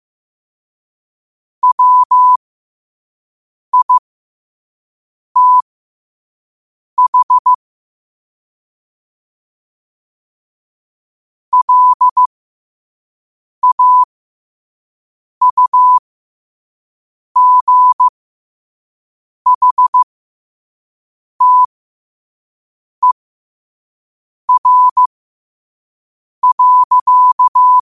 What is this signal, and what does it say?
Morse code, heard as a keyed tone, about 15 wpm, spaced as for 5 wpm, 1000 Hz. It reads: WITH LAUGHTER.
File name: qotdfiles / 5wpm